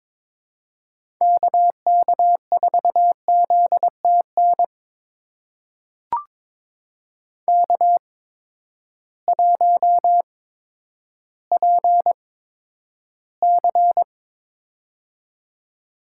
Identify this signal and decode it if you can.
Morse code, heard as a keyed tone, about 22 wpm, spaced as for 11 wpm, 700 Hz. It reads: KK4ZTN K1PC
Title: USA Call Signs - 8x Character Spacing - 22wpm